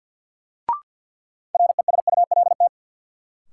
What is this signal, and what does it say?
Morse code, heard as a keyed tone, about 51 wpm, 700 Hz. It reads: RESULT
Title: Sentences from Top 700 Words - New Word Drill - 50wpm